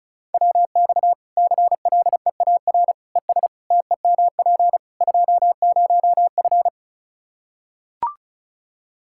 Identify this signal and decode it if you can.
Morse code, heard as a keyed tone, about 34 wpm, 700 Hz. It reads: WX CLEAR ES TEMP 20F
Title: Practice QSO - Exchange 2 - Info - 35wpm